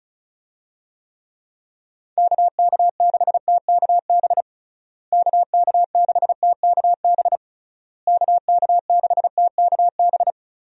Morse code, heard as a keyed tone, about 36 wpm, 700 Hz. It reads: KK6TKB KK6TKB KK6TKB
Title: USA Call Signs - 6 Characters - Repeated 3x - 35wpm